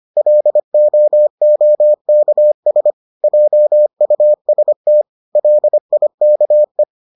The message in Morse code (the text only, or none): LOOKS JUST LIKE